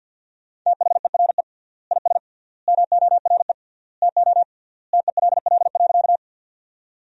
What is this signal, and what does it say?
Morse code, heard as a keyed tone, about 50 wpm, 700 Hz. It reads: THERE IS MORE TO TELL.
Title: 3x Word-Spacing - Sentences from Top 200 Words - 50wpm